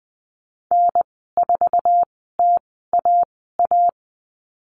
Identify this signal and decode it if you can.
Morse code, heard as a keyed tone, about 20 wpm, spaced as for 15 wpm, 700 Hz. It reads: N4TAA
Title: 082 Lesson 14g - Number 2 - Call Signs 20wpm